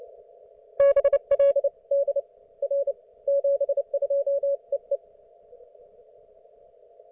Morse code, heard as a keyed tone, about 27 wpm, 560 Hz. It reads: BLD R 72 EE